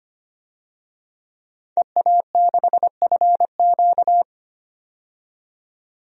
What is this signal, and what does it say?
Morse code, heard as a keyed tone, about 25 wpm, 700 Hz. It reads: EA6FQ
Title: DXCC-100 Call Signs - 5 Characters - 25wpm